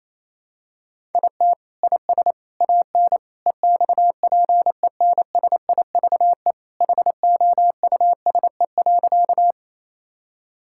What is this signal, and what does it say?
Morse code, heard as a keyed tone, about 28 wpm, 700 Hz. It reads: IT IS AN EXPENSIVE HOUSE.
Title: Sentences from Top 1000 Words - Review Part 6 - 28wpm